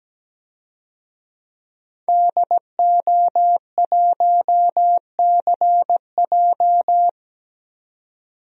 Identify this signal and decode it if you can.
Morse code, heard as a keyed tone, about 17 wpm, 700 Hz. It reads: DO1CJ
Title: DXCC-100 Call Signs - 5 Characters - 17wpm